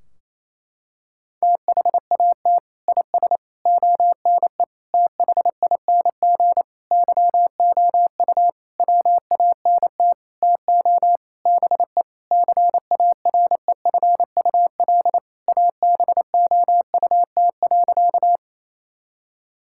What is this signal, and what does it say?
Morse code, heard as a keyed tone, about 28 wpm, 700 Hz. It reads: THAT IS ONE THING YOU WANT TO BE CAREFUL ABOUT.
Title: Sentences from Top 1600 Words 28wpm